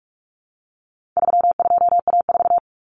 Morse code, heard as a keyed tone, about 44 wpm, 700 Hz. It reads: 32A4
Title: Common Cut Numbers - 4-Digit 45wpm